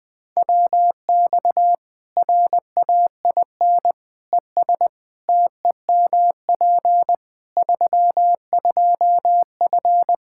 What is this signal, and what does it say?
Morse code, heard as a keyed tone, about 21 wpm, 700 Hz. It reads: WX RAIN ES TEMP 32F
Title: K1USN SST QSO Practice 20wpm